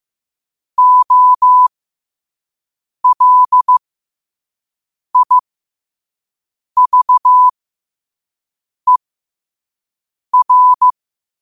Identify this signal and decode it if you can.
Morse code, heard as a keyed tone, about 15 wpm, spaced as for 5 wpm, 1000 Hz. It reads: OLIVER
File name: qotdfiles / 5wpm